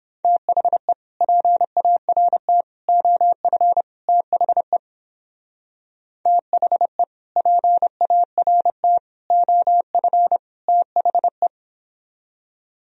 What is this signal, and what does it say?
Morse code, heard as a keyed tone, about 29 wpm, 700 Hz. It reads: THE PART OF THE THE PART OF THE